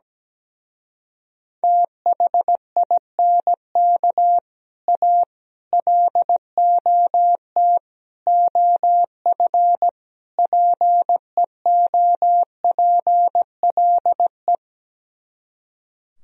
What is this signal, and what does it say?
Morse code, heard as a keyed tone, about 17 wpm, 700 Hz. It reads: THINK A LOT OF PEOPLE
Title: Speed-Racing - Sets of 5 Words 17wpm